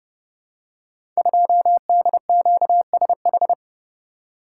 Morse code, heard as a keyed tone, about 30 wpm, 700 Hz. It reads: 2DQSH
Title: Code Groups - 5 Letter:Numbers - 30wpm